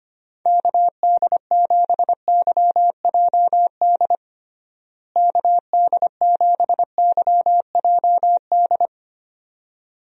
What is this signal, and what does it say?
Morse code, heard as a keyed tone, about 25 wpm, 700 Hz. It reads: KD7YJD KD7YJD
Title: USA Call Signs - 6 Characters - Repeated 2x - 25wpm